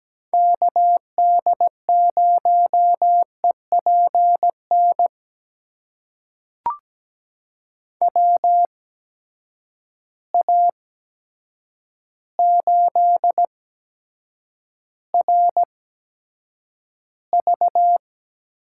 Morse code, heard as a keyed tone, about 17 wpm, spaced as for 8 wpm, 700 Hz. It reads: KD0EPN WA8RV
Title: USA Call Signs - 6 Characters - 8x Character Spacing - 17wpm